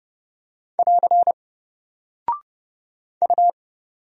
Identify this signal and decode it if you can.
Morse code, heard as a keyed tone, about 30 wpm, 700 Hz. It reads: <AR> U
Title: Single Character (Letter:Number:Punctuation:Prosign) 30wpm